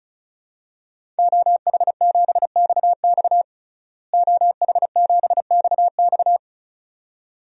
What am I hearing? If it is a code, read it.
Morse code, heard as a keyed tone, about 36 wpm, 700 Hz. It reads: OH7XX OH7XX
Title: DXCC-100 Call Signs - 5 Characters - Repeated 2x - 35wpm